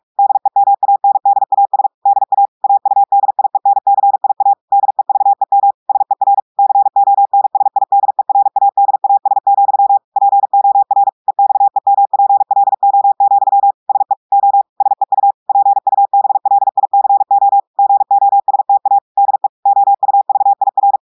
Morse code, heard as a keyed tone, about 46 wpm, 800 Hz. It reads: DEMANDAS DA AUDIENCIA DEVEM SER CONSIDERADAS, POR EXEMPLO, SE O SEU PUBLICO GOSTA DE OUVIR
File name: audios / portugais / portugais texte en clair 2 vitesse 45 mots minutes